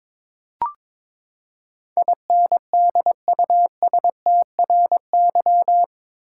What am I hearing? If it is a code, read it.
Morse code, heard as a keyed tone, about 22 wpm, 700 Hz. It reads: INDUSTRY